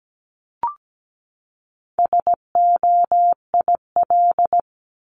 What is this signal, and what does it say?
Morse code, heard as a keyed tone, about 17 wpm, 700 Hz. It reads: SOIL
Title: Top 501-750 Words 17wpm